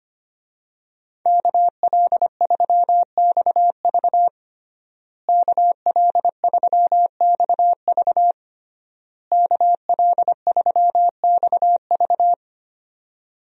Morse code, heard as a keyed tone, about 25 wpm, 700 Hz. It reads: KL3XV KL3XV KL3XV
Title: USA Call Signs - 5 Characters - Repeated 3x - 25wpm